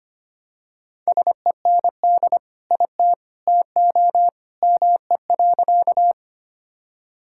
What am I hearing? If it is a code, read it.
Morse code, heard as a keyed tone, about 25 wpm, 700 Hz. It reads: SEND IT TO ME.
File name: practice / 25wpm / Sentences with 2-Word Phrases - 25wpm